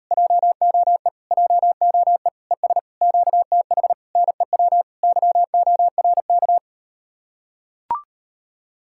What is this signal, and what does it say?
Morse code, heard as a keyed tone, about 38 wpm, 700 Hz. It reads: JOE JOE ES QTH NEW YORK